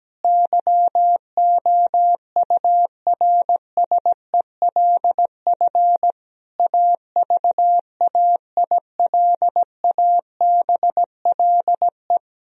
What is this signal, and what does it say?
Morse code, heard as a keyed tone, about 17 wpm, 700 Hz. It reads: YOURSELF AVAILABLE
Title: Sentences from Top 1000 Words - Review Part 3 - 17wpm